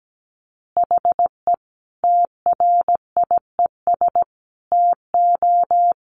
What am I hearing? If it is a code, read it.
Morse code, heard as a keyed tone, about 17 wpm, 700 Hz. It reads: HE TRIES TO
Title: Easy Phrases 17wpm